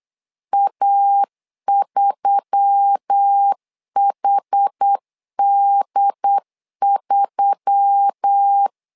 Morse code, heard as a keyed tone, about 8 wpm, 800 Hz. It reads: A3HD3
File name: LEZIONI / 5 LBVC / 5 L 8wpm 2831